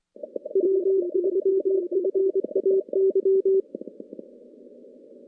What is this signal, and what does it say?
Morse code, heard as a keyed tone, about 25 wpm, 380 Hz. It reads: W3FAY
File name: contests / CW-SS-k5tr- -2300